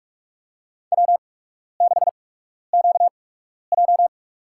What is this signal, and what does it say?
Morse code, heard as a keyed tone, about 46 wpm, spaced as for 12 wpm, 700 Hz. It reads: W6QJ
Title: USA Call Signs - 8x Character Spacing - 45wpm